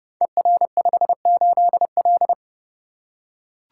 Morse code, heard as a keyed tone, about 30 wpm, 700 Hz. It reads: ER58L
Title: Code Groups - 5 Letter:Numbers - 30wpm